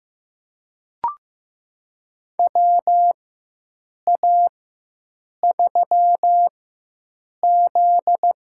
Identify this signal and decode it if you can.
Morse code, heard as a keyed tone, about 15 wpm, spaced as for 7 wpm, 700 Hz. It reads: WA3Z
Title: USA Call Signs - 6 Characters - 4x Character Spacing - 15wpm